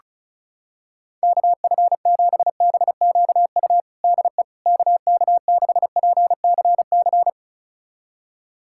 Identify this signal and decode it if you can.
Morse code, heard as a keyed tone, about 36 wpm, 700 Hz. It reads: KF7BQU DE KK6PCC
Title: USA Callsign DE Callsign 35wpm